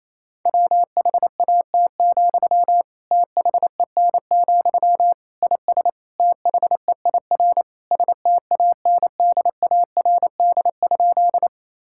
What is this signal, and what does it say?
Morse code, heard as a keyed tone, about 27 wpm, 700 Hz. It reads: WHAT, THEN, IS THEIR STANDARD?
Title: Sentences from Top 1000 Words - Review Part 6 - 28wpm